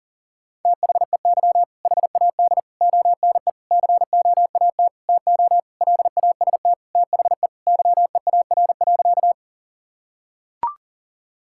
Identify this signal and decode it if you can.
Morse code, heard as a keyed tone, about 40 wpm, 700 Hz. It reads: THEY HAD ONE COAT TO LAST THE YEAR.
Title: Sentences from Top 1600 Words 40wpm